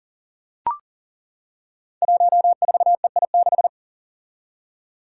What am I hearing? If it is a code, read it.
Morse code, heard as a keyed tone, about 39 wpm, 700 Hz. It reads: 14EI6
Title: Five Challenging Letter-Number Combinations 40wpm